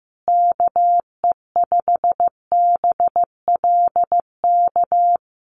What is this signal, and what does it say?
Morse code, heard as a keyed tone, about 15 wpm, 700 Hz. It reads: KE5BLK
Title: USA Call Signs Repeated 2x 15wpm